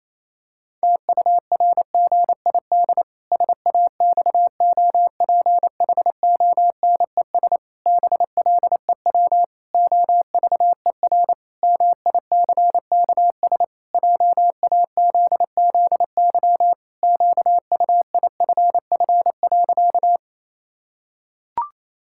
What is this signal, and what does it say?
Morse code, heard as a keyed tone, about 27 wpm, 700 Hz. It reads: TURGID SAXOPHONES BLEW OVER MICKS JAZZY QUIFF.
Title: Pangram Challenge v2 - 28wpm